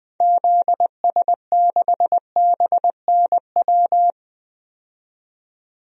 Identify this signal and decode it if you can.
Morse code, heard as a keyed tone, about 20 wpm, 700 Hz. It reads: ZS6BNW
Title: DXCC-100 Call Signs - 6 Characters - 20wpm